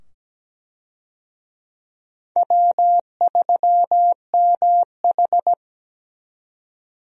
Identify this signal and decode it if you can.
Morse code, heard as a keyed tone, about 17 wpm, 700 Hz. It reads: W3MH
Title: USA Call Signs - 4 Characters - 17wpm